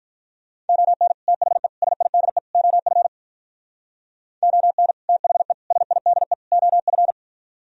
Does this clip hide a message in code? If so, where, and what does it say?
Yes: Morse code from 0.7 to 7.1 s, heard as a keyed tone, about 52 wpm, 700 Hz. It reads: ON THE SIDE OF ON THE SIDE OF